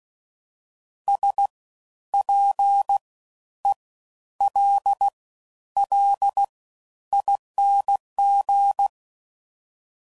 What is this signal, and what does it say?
Morse code, heard as a keyed tone, about 16 wpm, spaced as for 10 wpm, 800 Hz. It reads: SPELLING
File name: Suffixes / Special INGs / Fast Ings